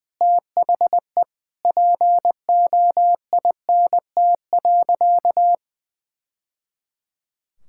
Morse code, heard as a keyed tone, about 21 wpm, 700 Hz. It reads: THE POINT.
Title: Sentences from Top 1000 Words - Review Part 6 - 20wpm